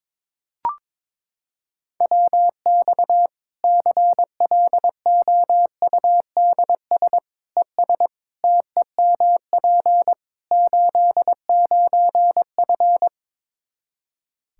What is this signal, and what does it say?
Morse code, heard as a keyed tone, about 22 wpm, 700 Hz. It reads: WX CLOUDS ES TEMP 89F
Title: Practice QSO - Exchange 2 - Info - 22wpm